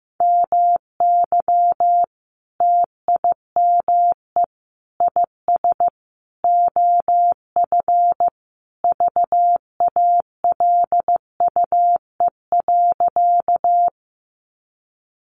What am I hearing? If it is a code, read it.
Morse code, heard as a keyed tone, about 15 wpm, 700 Hz. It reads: MY TIME IS OF VALUE.
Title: Sentences from Top 600 Words 15wpm